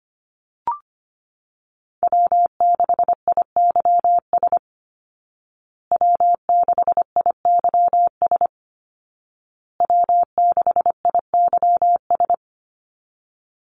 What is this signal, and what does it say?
Morse code, heard as a keyed tone, about 25 wpm, 700 Hz. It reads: W6IYS W6IYS W6IYS
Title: Call Signs Repeated 2x - 25wpm